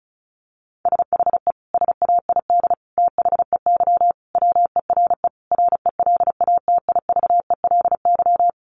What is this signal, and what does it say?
Morse code, heard as a keyed tone, about 34 wpm, 700 Hz. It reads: SHE SAID THEY WERE RELATIVELY
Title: Sentences from Top 1100 Words 35wpm